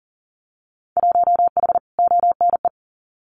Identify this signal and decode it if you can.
Morse code, heard as a keyed tone, about 40 wpm, 700 Hz. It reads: 1H ONE
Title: Winter Field Day Exchanges 40wpm